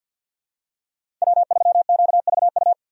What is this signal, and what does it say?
Morse code, heard as a keyed tone, about 53 wpm, 700 Hz. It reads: W3QVU
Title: USA Call Signs - 5 Characters - 50wpm